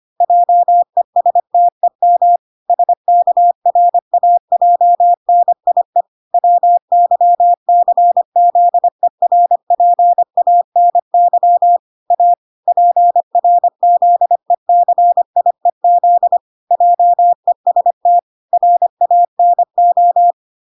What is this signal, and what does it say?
Morse code, heard as a keyed tone, about 25 wpm, 700 Hz. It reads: JESTEM SKRAJNIE WYCZERPANY A PRZECIEZ JEST RANO